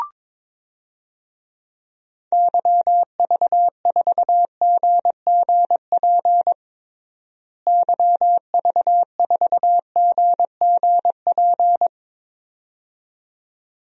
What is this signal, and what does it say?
Morse code, heard as a keyed tone, about 22 wpm, 700 Hz. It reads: YV4GGP YV4GGP
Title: DXCC-100 Call Signs - 6 Characters - Repeated 2x - 22wpm